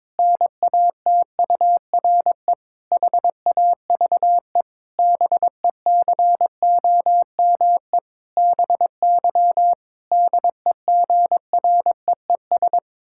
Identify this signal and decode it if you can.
Morse code, heard as a keyed tone, about 22 wpm, 700 Hz. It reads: NATURE HAVE BECOME BY DEGREES
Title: Pangram Challenge v2 - 22wpm